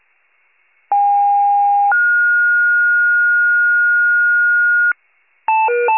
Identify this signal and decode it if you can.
Two-tone page, heard as a one-second tone, then a three-second tone, about 810 then 1430 Hz, with noise. It is heard from 0.9 to 4.9 s.
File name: pager / 73 FIRE / wcalert